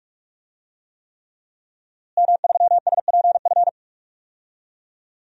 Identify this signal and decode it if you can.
Morse code, heard as a keyed tone, about 47 wpm, 700 Hz. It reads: M3SPF
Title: DXCC-100 Call Signs - 5 Characters - Repeated 3x - 45wpm